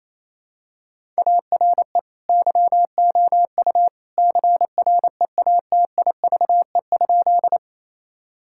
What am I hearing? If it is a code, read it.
Morse code, heard as a keyed tone, about 28 wpm, 700 Hz. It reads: ARE YOU CREATIVE?
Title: Sentences from Top 1000 Words - Review Part 5 - 28wpm